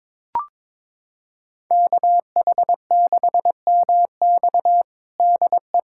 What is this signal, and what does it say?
Morse code, heard as a keyed tone, about 22 wpm, 700 Hz. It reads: KH6MX DE
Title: USA Callsign DE Callsign - 5 Characters - 22wpm